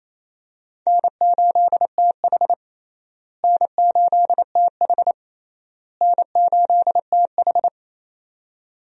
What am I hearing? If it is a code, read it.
Morse code, heard as a keyed tone, about 28 wpm, 700 Hz. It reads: N8TH N8TH N8TH
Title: USA Call Signs - 4 Characters - Repeated 3x - 28wpm